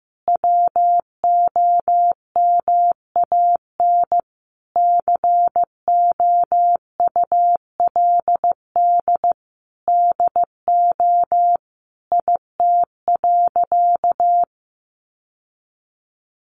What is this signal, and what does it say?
Morse code, heard as a keyed tone, about 15 wpm, 700 Hz. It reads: WOMAN COULD DO IT.
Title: Sentences from Top 1100 Words 15wpm